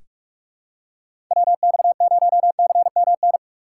Morse code, heard as a keyed tone, about 46 wpm, 700 Hz. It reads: WX0CMN
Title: USA Call Signs - 6 Characters - 3x Character Spacing - 45wpm